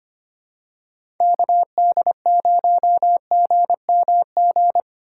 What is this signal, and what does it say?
Morse code, heard as a keyed tone, about 25 wpm, 700 Hz. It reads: KD0GMG
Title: USA Call Signs - 6 Characters - 3x Character Spacing - 25wpm